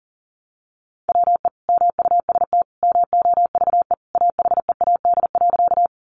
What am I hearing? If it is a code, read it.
Morse code, heard as a keyed tone, about 40 wpm, 700 Hz. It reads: WE MUST MOVE AHEAD.
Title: Sentences from Top 1100 Words 40wpm